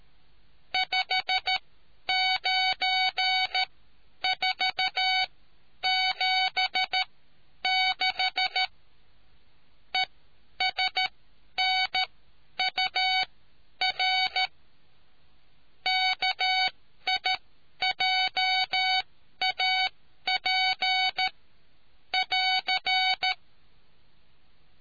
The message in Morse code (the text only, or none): 59476 ESNUR KIJAP <AR>